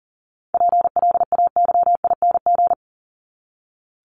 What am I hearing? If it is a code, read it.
Morse code, heard as a keyed tone, about 40 wpm, 700 Hz. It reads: PLAYING
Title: ING Suffix - Top 100 Words with Base Words 40wpm